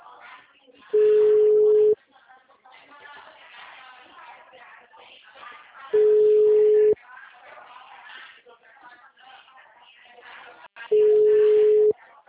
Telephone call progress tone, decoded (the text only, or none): ringback tone